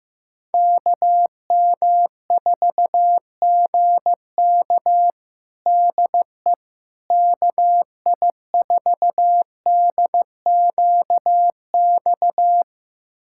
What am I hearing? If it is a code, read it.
Morse code, heard as a keyed tone, about 15 wpm, 700 Hz. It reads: KM4GK DE KI4DQX
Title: USA Callsign DE Callsign 15wpm